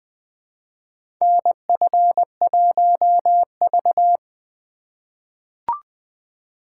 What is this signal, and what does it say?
Morse code, heard as a keyed tone, about 20 wpm, 700 Hz. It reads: NF1V
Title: USA Call Signs - 4 Characters - 2x Character Spacing - 20wpm